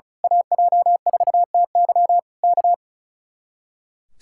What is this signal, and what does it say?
Morse code, heard as a keyed tone, about 34 wpm, 700 Hz. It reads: AJ4TY K
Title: Speed-Racing Common QSO Phrases - 35wpm